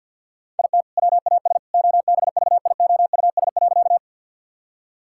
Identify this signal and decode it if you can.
Morse code, heard as a keyed tone, about 50 wpm, 700 Hz. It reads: IT WAS OBVIOUS.
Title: Sentences from Top 1600 Words 50wpm